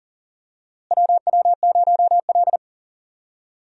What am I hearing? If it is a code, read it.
Morse code, heard as a keyed tone, about 40 wpm, 700 Hz. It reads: WW0L
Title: USA Call Signs - 4 Characters - Repeated 2x - 40wpm